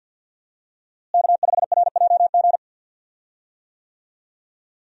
Morse code, heard as a keyed tone, about 50 wpm, 700 Hz. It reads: K5RJG
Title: USA Call Signs - 5 Characters - Repeated 3x - 50wpm